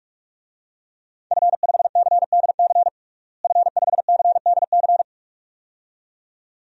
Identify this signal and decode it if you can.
Morse code, heard as a keyed tone, about 46 wpm, 700 Hz. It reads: F5CDC F5CDC
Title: DXCC-100 Call Signs - 5 Characters - Repeated 2x - 45wpm